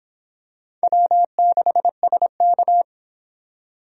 Morse code, heard as a keyed tone, about 25 wpm, 700 Hz. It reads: W6SK